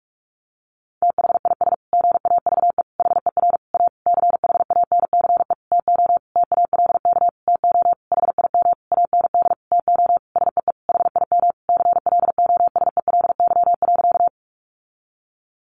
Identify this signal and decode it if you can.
Morse code, heard as a keyed tone, about 45 wpm, 700 Hz. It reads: THIS GAVE HER A CHANCE TO TALK TO HIM AND TO SEE HIM CLOSELY.
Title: Sentences from Top 1400 Words 45wpm